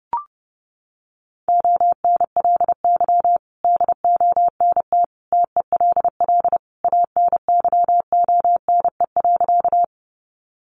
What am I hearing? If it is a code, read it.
Morse code, heard as a keyed tone, about 30 wpm, 700 Hz. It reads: ONLY DONT TELL ANYONE.
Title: Sentences from Top 900 Words 30wpm